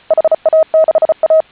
Morse code, heard as a keyed tone, about 34 wpm, 630 Hz. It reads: HA6A
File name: HA6A CW